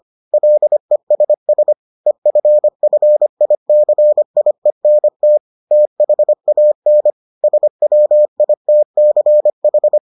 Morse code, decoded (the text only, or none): LESS EFFICIENT THAN SWITCH